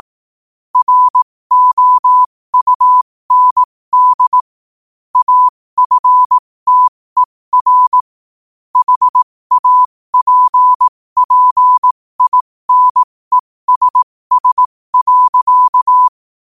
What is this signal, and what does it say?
Morse code, heard as a keyed tone, about 18 wpm, spaced as for 15 wpm, 1000 Hz. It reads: ROUND AFTER HAPPINESS.